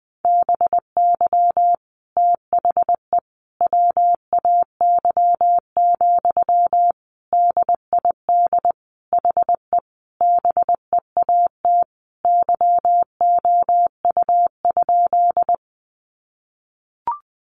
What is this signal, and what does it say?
Morse code, heard as a keyed tone, about 20 wpm, 700 Hz. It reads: BY THE WAY, DID HE BEAT YOU?
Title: Sentences from Top 1300 Words 20wpm